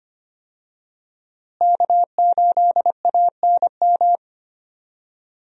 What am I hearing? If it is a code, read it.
Morse code, heard as a keyed tone, about 25 wpm, 700 Hz. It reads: K8ANM
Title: USA Call Signs - 5 Characters - Repeated 3x - 25wpm